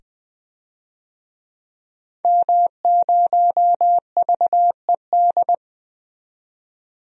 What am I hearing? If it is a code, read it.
Morse code, heard as a keyed tone, about 20 wpm, 700 Hz. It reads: M0VED
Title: DXCC-100 Call Signs - 5 Characters - 20wpm